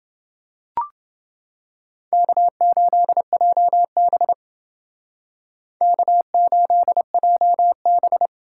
Morse code, heard as a keyed tone, about 28 wpm, 700 Hz. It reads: K8JB K8JB